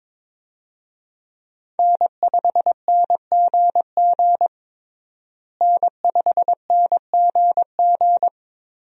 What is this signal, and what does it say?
Morse code, heard as a keyed tone, about 22 wpm, 700 Hz. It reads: N5NGG N5NGG
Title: USA Call Signs - 5 Characters - Repeated 3x - 22wpm